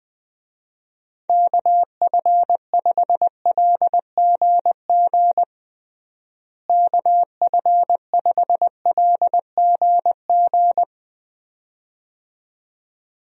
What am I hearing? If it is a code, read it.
Morse code, heard as a keyed tone, about 20 wpm, 700 Hz. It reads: KF5LGG KF5LGG
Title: USA Call Signs - 6 Characters - Repeated 2x - 20wpm